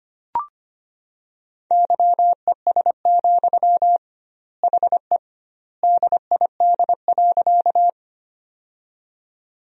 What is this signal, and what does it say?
Morse code, heard as a keyed tone, about 25 wpm, 700 Hz. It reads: YES, HE DID.
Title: 2x Word-Spacing - Sentences from Top 500 Words - 25wpm